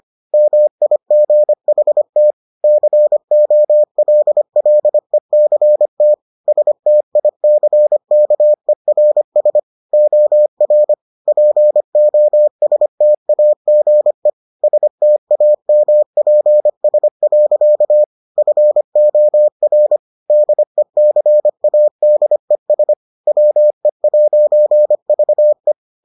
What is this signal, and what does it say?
Morse code, heard as a keyed tone, about 25 wpm, 600 Hz. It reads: MIGHT COLLECT STICKERS OR POSTAGE STAMPS. FOR DECADES WE'VE